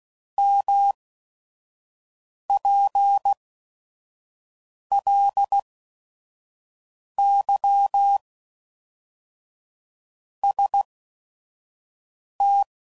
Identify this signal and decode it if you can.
Morse code, heard as a keyed tone, about 16 wpm, spaced as for 5 wpm, 800 Hz. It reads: MPLY ST
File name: CW Code Groups @5WPM / Technical Subjects / XC